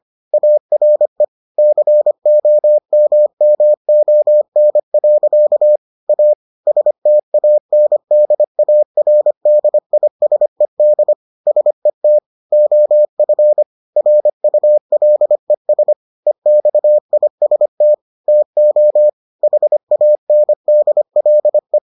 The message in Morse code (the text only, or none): ARE COMMON. A STANDARDISED SET OF RULES EXIST TO HANDLE